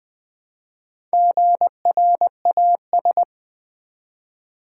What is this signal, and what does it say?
Morse code, heard as a keyed tone, about 20 wpm, 700 Hz. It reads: GRAS
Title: Build A Word - 1 Letter At A Time - 5-Letter Words - 20wpm